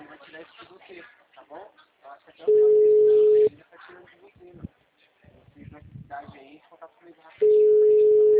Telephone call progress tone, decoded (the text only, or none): ringback tone